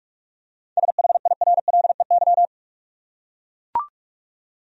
Morse code, heard as a keyed tone, about 44 wpm, 700 Hz. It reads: SHIRLEY
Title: USA Common Names - 45wpm